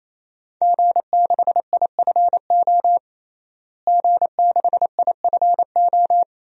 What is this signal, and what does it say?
Morse code, heard as a keyed tone, about 27 wpm, 700 Hz. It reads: G6IFO G6IFO
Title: DXCC-100 Call Signs - 5 Characters - Repeated 3x - 28wpm